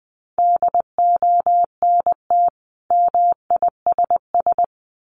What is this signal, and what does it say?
Morse code, heard as a keyed tone, about 20 wpm, 700 Hz. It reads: DONT MISS
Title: Sentences from Top 800 Words 20wpm